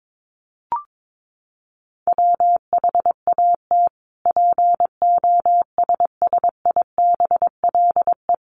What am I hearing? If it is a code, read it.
Morse code, heard as a keyed tone, about 22 wpm, 700 Hz. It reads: WHAT POSSIBLE
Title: Sentences from Top 1300 Words 22wpm